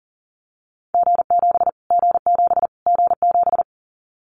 Morse code, heard as a keyed tone, about 40 wpm, 700 Hz. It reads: G7 G7 G7
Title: Callsign Prefixes for Top 25 DXCC - Repeated 3x - 40wpm